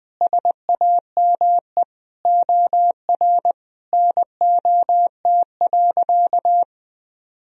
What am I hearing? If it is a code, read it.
Morse code, heard as a keyed tone, about 20 wpm, 700 Hz. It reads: SAME OR NOT.
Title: Sentences from Top 1000 Words - Review Part 4 - 20wpm